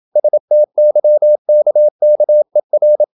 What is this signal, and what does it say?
Morse code, heard as a keyed tone, about 28 wpm, 600 Hz. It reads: STYKKER
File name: Ferie QRQ 27wpm